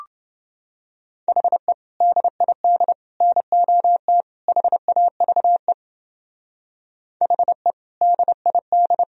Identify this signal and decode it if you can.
Morse code, heard as a keyed tone, about 29 wpm, 700 Hz. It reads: HE DID NOT HAVE HE DID